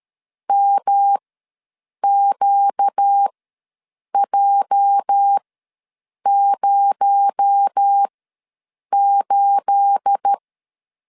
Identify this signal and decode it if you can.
Morse code, heard as a keyed tone, about 13 wpm, spaced as for 7 wpm, 800 Hz. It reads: MQJ08
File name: LEZIONI / 9 2867 ws 12wpm 3431